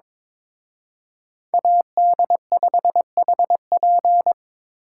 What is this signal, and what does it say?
Morse code, heard as a keyed tone, about 26 wpm, 700 Hz. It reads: AD5HP
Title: Speed Racing - USA 2x2 Call Signs 22wpm